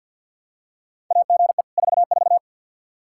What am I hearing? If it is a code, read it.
Morse code, heard as a keyed tone, about 52 wpm, 700 Hz. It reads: AGE 44
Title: Practice QSO - Exchange 3 - More Info - 50wpm